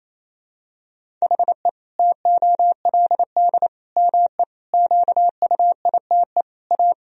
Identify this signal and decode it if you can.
Morse code, heard as a keyed tone, about 28 wpm, 700 Hz. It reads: HE TOLD ME QUITE A
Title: Sentences from Top 1000 Words - Review Part 3 - 28wpm